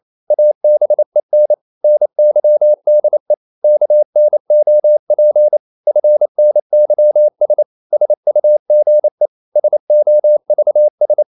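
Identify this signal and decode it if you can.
Morse code, heard as a keyed tone, about 28 wpm, 600 Hz. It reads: ABEN NYDE KNOP FNYS SUGE SOVS
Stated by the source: Danish short words sent in Morse at 28 wpm, from CW med Gnister.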